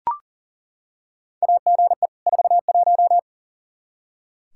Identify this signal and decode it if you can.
Morse code, heard as a keyed tone, about 40 wpm, 700 Hz. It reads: AGE 41
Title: K1USN SST QSO Practice 40wpm